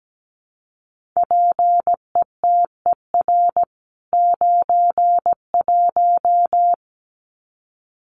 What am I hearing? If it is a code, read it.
Morse code, heard as a keyed tone, about 17 wpm, 700 Hz. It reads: PETER 91
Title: CWT Exchange 17wpm